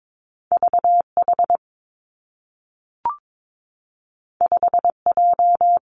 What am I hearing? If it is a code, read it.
Morse code, heard as a keyed tone, about 22 wpm, 700 Hz. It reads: V H 5 J
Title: Two Challenging Letter-Number Combinations 22wpm